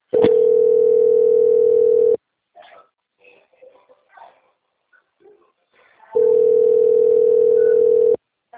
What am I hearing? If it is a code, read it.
Ringback tone (North America), 440 + 480 Hz.
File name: out- - - - - .25201